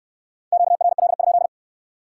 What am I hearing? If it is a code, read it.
Morse code, heard as a keyed tone, about 69 wpm, 700 Hz. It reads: QRL?